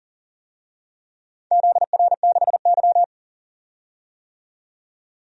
Morse code, heard as a keyed tone, about 40 wpm, 700 Hz. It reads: ZR6Y
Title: DXCC-100 Call Signs - 4 Characters - 40wpm